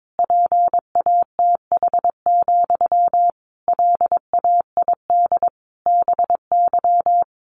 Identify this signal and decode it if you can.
Morse code, heard as a keyed tone, about 22 wpm, 700 Hz. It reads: PATH, LAID BY
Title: A Sound Of Thunder 22wpm